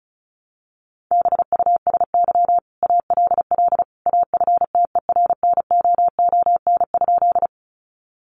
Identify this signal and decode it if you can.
Morse code, heard as a keyed tone, about 34 wpm, 700 Hz. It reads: BUSY ALL AFTERNOON?
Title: Sentences from Top 1500 Words 35wpm